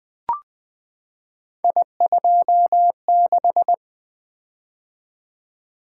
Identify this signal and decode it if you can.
Morse code, heard as a keyed tone, about 20 wpm, 700 Hz. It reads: I26
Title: Three Letter-Number Combinations - 20wpm